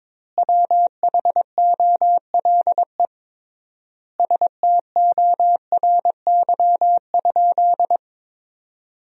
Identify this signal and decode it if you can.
Morse code, heard as a keyed tone, about 21 wpm, 700 Hz. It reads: WHOLE STORY?
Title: 3x Word-Spacing - Sentences from Top 400 Words - 22wpm